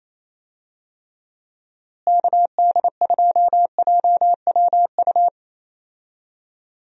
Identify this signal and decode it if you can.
Morse code, heard as a keyed tone, about 28 wpm, 700 Hz. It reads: KD2JWU
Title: USA Call Signs - 6 Characters - Repeated 3x - 28wpm